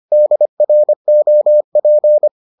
Morse code, heard as a keyed tone, about 25 wpm, 600 Hz. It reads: DROP